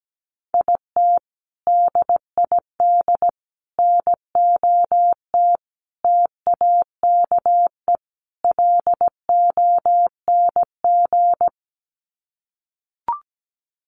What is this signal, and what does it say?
Morse code, heard as a keyed tone, about 17 wpm, 700 Hz. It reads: IT DID NOT TAKE LONG
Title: Sets of 5 Words Encore2 17wpm